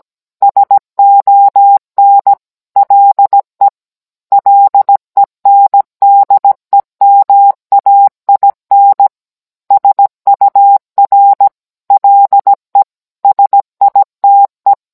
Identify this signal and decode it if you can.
Morse code, heard as a keyed tone, about 17 wpm, 800 Hz. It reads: SON LE LENDEMAIN SUR LE SITE